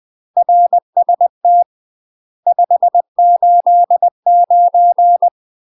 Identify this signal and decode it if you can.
Morse code, heard as a keyed tone, about 20 wpm, 700 Hz. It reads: RST 589
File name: UK MASTER QSOs TRAINING / 20WPM / UK MASTER QSOs 0047